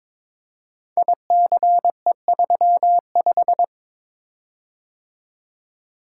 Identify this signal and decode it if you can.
Morse code, heard as a keyed tone, about 22 wpm, 700 Hz. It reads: ICE35
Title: Code Groups - 5 Letter:Numbers - 22wpm